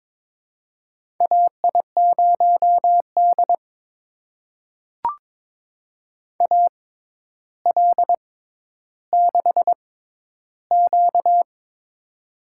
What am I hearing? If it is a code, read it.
Morse code, heard as a keyed tone, about 22 wpm, spaced as for 11 wpm, 700 Hz. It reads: AI0D AL6Q
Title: USA Call Signs - 4 Characters - 6x Character Spacing - 22wpm